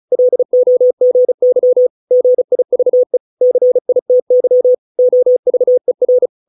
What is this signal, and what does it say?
Morse code, heard as a keyed tone, about 35 wpm, 500 Hz. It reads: LOGY GIVE CITY OVER